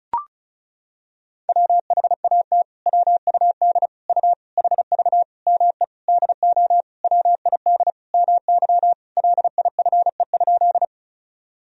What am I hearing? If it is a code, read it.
Morse code, heard as a keyed tone, about 36 wpm, 700 Hz. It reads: WHAT WUD U HV ME DO WID MY LIFE?
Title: Sentences from Top 300 Words and Abbreviations 35wpm